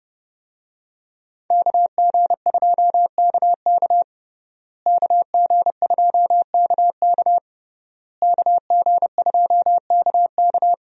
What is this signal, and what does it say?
Morse code, heard as a keyed tone, about 32 wpm, 700 Hz. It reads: KG2KK KG2KK KG2KK
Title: USA Call Signs - 5 Characters - Repeated 3x - 30wpm